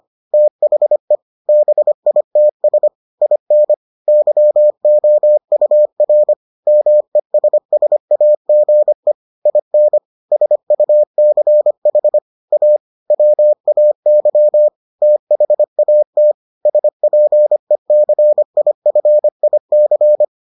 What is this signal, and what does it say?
Morse code, heard as a keyed tone, about 25 wpm, 600 Hz. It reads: THE BITS IN YOUR MESSAGE IN SUCH A WAY THAT SPECIFIC